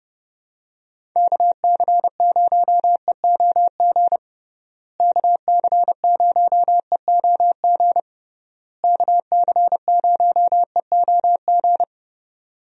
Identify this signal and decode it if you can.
Morse code, heard as a keyed tone, about 32 wpm, 700 Hz. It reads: KC0EOG KC0EOG KC0EOG
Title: USA Call Signs - 6 Characters - Repeated 3x - 30wpm